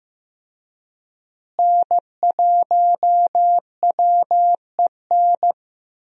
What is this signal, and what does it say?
Morse code, heard as a keyed tone, about 15 wpm, 700 Hz. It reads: N1WEN
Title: USA Call Signs - 5 Characters - Repeated 2x - 15wpm